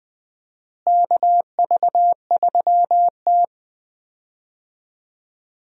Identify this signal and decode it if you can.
Morse code, heard as a keyed tone, about 20 wpm, 700 Hz. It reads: KV3T
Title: USA Call Signs - 4 Characters - Repeated 2x - 20wpm